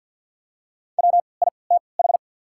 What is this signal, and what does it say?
Morse code, heard as a keyed tone, about 53 wpm, spaced as for 29 wpm, 700 Hz. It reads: WITH
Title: 3x Character Spacing - Top 200 Words - 50wpm